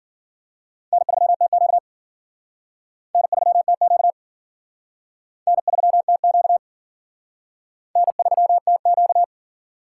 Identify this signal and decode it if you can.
Morse code, heard as a keyed tone, about 51 wpm, 700 Hz. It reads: N3TQ N3TQ N3TQ N3TQ